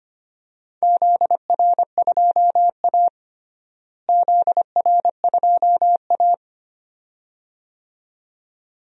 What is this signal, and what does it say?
Morse code, heard as a keyed tone, about 25 wpm, 700 Hz. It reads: ZR2A ZR2A
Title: DXCC-100 Call Signs - 4 Characters - Repeated 2x - 25wpm